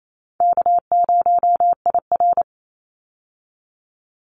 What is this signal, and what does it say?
Morse code, heard as a keyed tone, about 28 wpm, 700 Hz. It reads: K0IR
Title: Call Signs - 28wpm